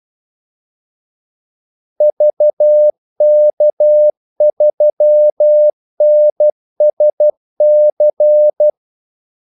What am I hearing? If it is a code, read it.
Morse code, heard as a keyed tone, about 12 wpm, 600 Hz. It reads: VK3NSC
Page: 2 ekstra ordmellemrum mellem kaldesignalerne